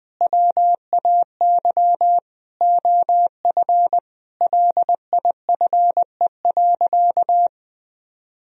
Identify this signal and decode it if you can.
Morse code, heard as a keyed tone, about 20 wpm, 700 Hz. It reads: WAY OF LIFE.
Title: Sentences from Top 1000 Words - Review Part 3 - 20wpm